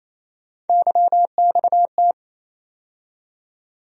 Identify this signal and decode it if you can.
Morse code, heard as a keyed tone, about 28 wpm, 700 Hz. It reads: YXT
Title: USA Call Sign Postfixes - 3 Letters 28wpm